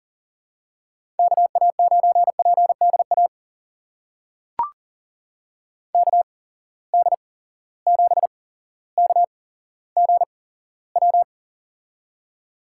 Morse code, heard as a keyed tone, about 40 wpm, spaced as for 17 wpm, 700 Hz. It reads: KA9PDA KD7KGW
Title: USA Call Signs - 6 Characters - 8x Character Spacing - 40wpm